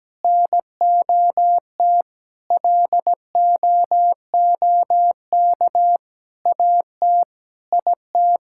Morse code, heard as a keyed tone, about 17 wpm, 700 Hz. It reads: NOT LOOK AT IT
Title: Sentences from Top 1000 Words - Review Part 6 - 17wpm